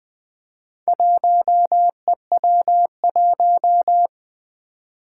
Morse code, heard as a keyed tone, about 21 wpm, 700 Hz. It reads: 1EW1
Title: Four Challenging Letter-Number Combinations 20wpm